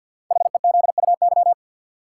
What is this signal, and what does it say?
Morse code, heard as a keyed tone, about 52 wpm, 700 Hz. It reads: HEZUY
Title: Code Groups - 5 Letters - 50wpm